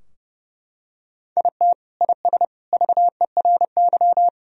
Morse code, heard as a keyed tone, about 30 wpm, 700 Hz. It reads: IT IS VERY
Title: Sentences from Top 1000 Words - Review Part 2 - 30wpm